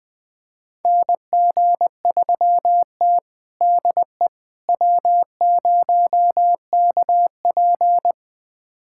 Morse code, heard as a keyed tone, about 20 wpm, 700 Hz. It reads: NG3T DE W0KP
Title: USA Callsign DE Callsign - 4 Characters - 20wpm